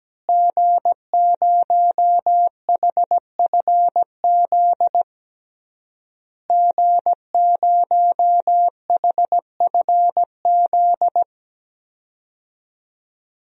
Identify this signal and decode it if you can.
Morse code, heard as a keyed tone, about 17 wpm, 700 Hz. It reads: G0HFZ G0HFZ
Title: DXCC-100 Call Signs - 5 Characters - Repeated 3x - 17wpm